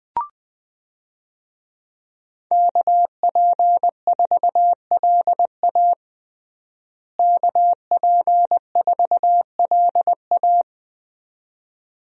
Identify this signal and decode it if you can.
Morse code, heard as a keyed tone, about 20 wpm, 700 Hz. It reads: KP4LA KP4LA
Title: USA Call Signs - 5 Characters - Repeated 2x - 20wpm